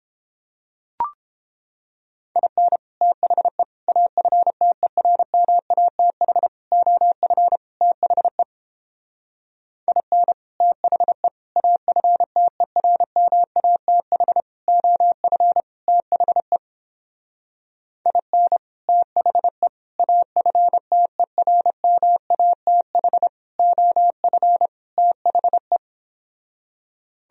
Morse code, heard as a keyed tone, about 29 wpm, 700 Hz. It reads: IN THE AFTERMATH OF THE IN THE AFTERMATH OF THE IN THE AFTERMATH OF THE